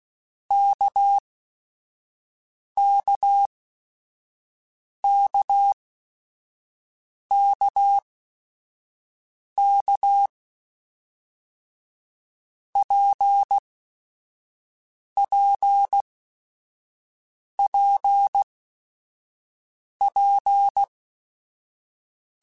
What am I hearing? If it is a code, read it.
Morse code, heard as a keyed tone, about 16 wpm, 800 Hz. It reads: K K K K K P P P P
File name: Basic Sets / Basic Letter Groups Collective7 / RKPX@5